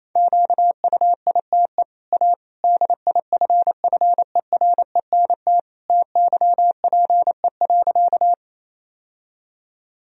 Morse code, heard as a keyed tone, about 27 wpm, 690 Hz. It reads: QUITE A DIFFERENT TYPE.
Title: Sentences from Top 1000 Words - Review Part 3 - 28wpm